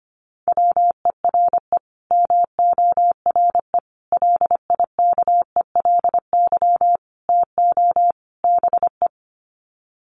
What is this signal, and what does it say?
Morse code, heard as a keyed tone, about 25 wpm, 700 Hz. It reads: WERE MORE LIKELY TO BE
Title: Sets of 5 Words Encore2 25wpm